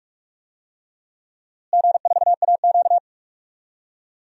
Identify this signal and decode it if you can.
Morse code, heard as a keyed tone, about 46 wpm, 690 Hz. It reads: G4AQ
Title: DXCC-100 Call Signs - 4 Characters - Repeated 3x - 45wpm